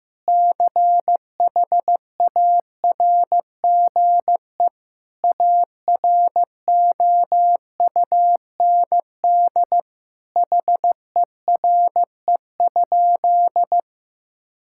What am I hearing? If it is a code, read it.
Morse code, heard as a keyed tone, about 15 wpm, 700 Hz. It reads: CHARGE AROUND HERE?
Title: Sentences from Top 1000 Words - Review Part 6 - 15wpm